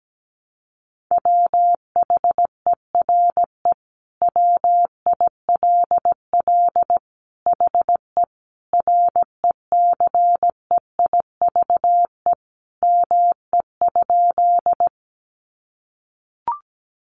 Morse code, heard as a keyed tone, about 17 wpm, 700 Hz. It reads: WHERE WILL HE RECEIVE ME?
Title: Sentences from Top 1000 Words 17wpm